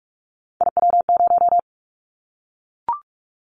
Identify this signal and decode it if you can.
Morse code, heard as a keyed tone, about 44 wpm, 700 Hz. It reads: IW0
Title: Callsign Prefixes for Top 25 DXCC - Repeated 3x - 45wpm